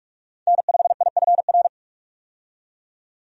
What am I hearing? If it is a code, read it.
Morse code, heard as a keyed tone, about 46 wpm, 700 Hz. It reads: N5IFR
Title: USA Call Signs - 5 Characters - 45wpm